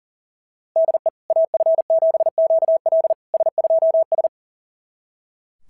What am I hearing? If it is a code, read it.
Morse code, heard as a keyed tone, about 40 wpm, 690 Hz. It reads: DE AF7QL S2S
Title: SOTA QSO Practice 40wpm